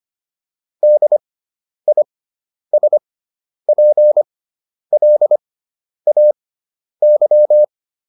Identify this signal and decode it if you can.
Morse code, heard as a keyed tone, about 25 wpm, spaced as for 10 wpm, 600 Hz. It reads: DISPLAY